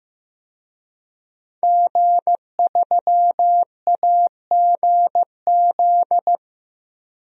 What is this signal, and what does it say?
Morse code, heard as a keyed tone, about 15 wpm, 700 Hz. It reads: G3AGZ
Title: DXCC-100 Call Signs - 5 Characters - 15wpm